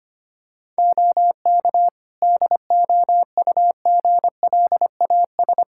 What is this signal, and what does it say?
Morse code, heard as a keyed tone, about 25 wpm, 700 Hz. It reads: OK DOUGLAS
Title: Practice QSO - Exchange 4 - Ending - 25wpm